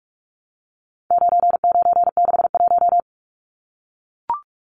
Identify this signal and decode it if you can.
Morse code, heard as a keyed tone, about 44 wpm, 700 Hz. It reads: 9961
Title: Numbers - 4-Digit 45wpm